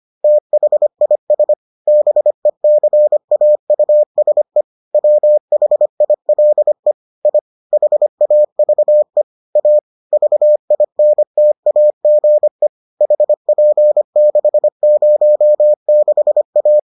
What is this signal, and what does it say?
Morse code, heard as a keyed tone, about 25 wpm, 600 Hz. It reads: THIS BECAUSE WHILE I HAVE A VINTAGE HP606A